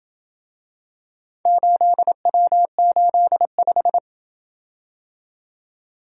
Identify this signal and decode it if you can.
Morse code, heard as a keyed tone, about 27 wpm, 700 Hz. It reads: 8W85